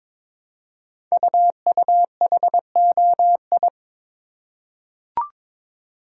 Morse code, heard as a keyed tone, about 22 wpm, 700 Hz. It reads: UUHOI
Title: Code Groups - 5 Letters - 22wpm